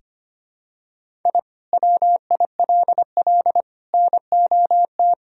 Morse code, heard as a keyed tone, about 25 wpm, 700 Hz. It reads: I WILL NOT
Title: Sentences from Top 1000 Words - Review Part 3 - 25wpm